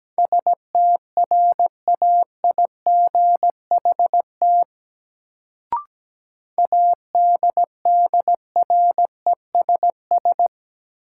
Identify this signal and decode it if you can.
Morse code, heard as a keyed tone, about 17 wpm, 700 Hz. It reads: STRAIGHT ADDRESS
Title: Sentences from Top 1000 Words - New Word Drill - 17wpm